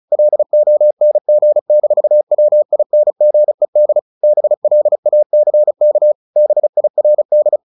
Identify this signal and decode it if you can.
Morse code, heard as a keyed tone, about 34 wpm, 600 Hz. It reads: LONG-WINGED BLACK BIRD
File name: the old man and the sea / the old man and the sea 1 35wpm